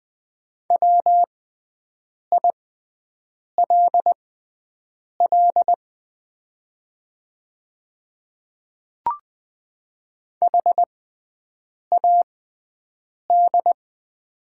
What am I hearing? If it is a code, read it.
Morse code, heard as a keyed tone, about 20 wpm, spaced as for 7 wpm, 690 Hz. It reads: WILL HAD
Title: 6x Character Spacing - Top 100 Words - 20wpm